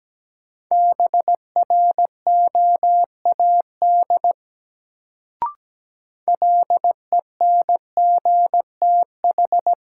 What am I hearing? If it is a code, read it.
Morse code, heard as a keyed tone, about 17 wpm, 700 Hz. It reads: BROAD LENGTH
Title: Sentences from Top 1000 Words - New Word Drill - 17wpm